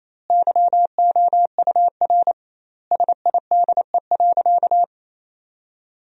Morse code, heard as a keyed tone, about 28 wpm, 700 Hz. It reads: YOUR SIDE.
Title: 2x Word-Spacing - Sentences from Top 400 Words - 28wpm